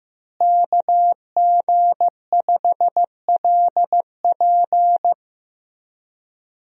Morse code, heard as a keyed tone, about 15 wpm, 700 Hz. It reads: KG5LP
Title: USA Call Signs - 5 Characters - Repeated 3x - 15wpm